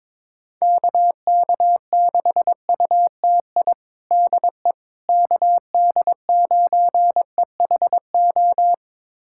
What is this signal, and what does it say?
Morse code, heard as a keyed tone, about 22 wpm, 700 Hz. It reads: KK6UTI DE KD9EHO
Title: USA Callsign DE Callsign 22wpm